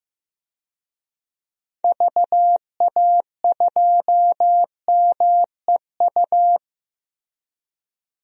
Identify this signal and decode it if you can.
Morse code, heard as a keyed tone, about 15 wpm, 700 Hz. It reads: VA2MEU
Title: DXCC-100 Call Signs - 6 Characters - Repeated 3x - 15wpm